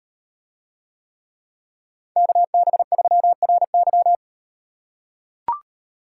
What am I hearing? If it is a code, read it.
Morse code, heard as a keyed tone, about 38 wpm, 700 Hz. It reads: KB3RY